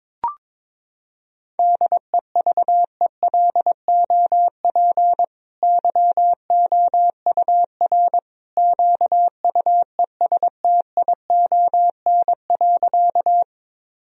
Morse code, heard as a keyed tone, about 21 wpm, 700 Hz. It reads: DEVELOP YOUR QUESTION.
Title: Sentences from Top 1000 Words - Review Part 2 - 22wpm